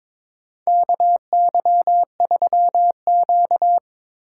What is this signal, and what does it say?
Morse code, heard as a keyed tone, about 22 wpm, 700 Hz. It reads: KY3Q
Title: USA Call Signs - 4 Characters - 4x Character Spacing - 22wpm